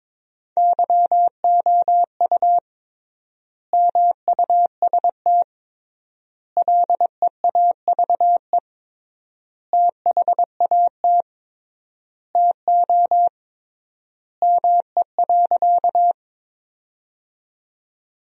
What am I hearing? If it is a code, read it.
Morse code, heard as a keyed tone, about 22 wpm, 700 Hz. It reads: YOU MUST LEAVE THAT TO ME.
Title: 3x Word-Spacing - Sentences from Top 400 Words - 22wpm